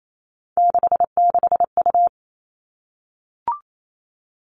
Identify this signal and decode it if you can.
Morse code, heard as a keyed tone, about 28 wpm, 700 Hz. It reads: 66U
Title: All Cut Numbers - 3-Digit 28wpm